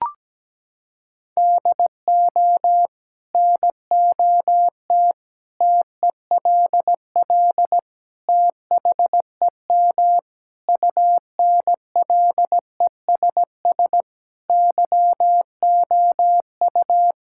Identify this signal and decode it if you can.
Morse code, heard as a keyed tone, about 17 wpm, 700 Hz. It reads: DO NOT TELL THEM UNLESS YOU
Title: Sentences from Top 1000 Words - Review Part 6 - 17wpm